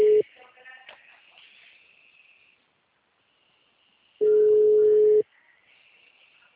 Ringback tone (Europe, 425 Hz).